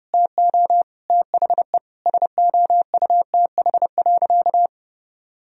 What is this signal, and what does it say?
Morse code, heard as a keyed tone, about 30 wpm, 700 Hz. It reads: TO THE SOUTH.
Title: Sentences from Top 400 Words and Abbreviations 30wpm